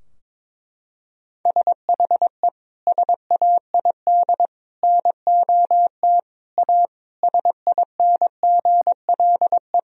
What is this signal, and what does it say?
Morse code, heard as a keyed tone, about 21 wpm, 700 Hz. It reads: SHE SAID NOT A SINGLE
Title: Sentences from Top 1000 Words - Review Part 4 - 22wpm